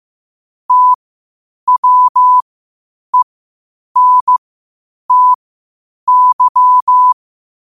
Morse code, heard as a keyed tone, about 15 wpm, spaced as for 8 wpm, 1000 Hz. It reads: TWENTY
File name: qotdfiles / 8wpm